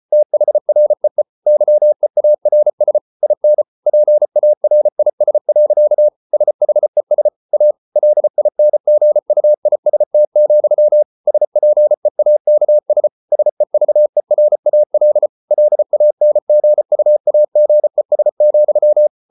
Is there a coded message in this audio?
Yes: Morse code from 0.1 to 19.1 s, heard as a keyed tone, about 34 wpm, 600 Hz. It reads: THREE YEARS IN PARIS. SHES A LINGUIST, SPEAKS SEVERAL LANGUAGES,